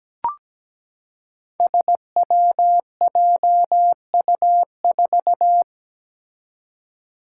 Morse code, heard as a keyed tone, about 17 wpm, 700 Hz. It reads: SWJU4
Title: Five Challenging Letter-Number Combinations 17wpm